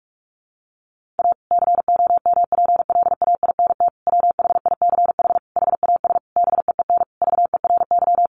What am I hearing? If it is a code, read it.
Morse code, heard as a keyed tone, about 44 wpm, 700 Hz. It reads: A COMPLAINT WHICH HAS BEEN VERY
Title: Sentences from Top 1400 Words 45wpm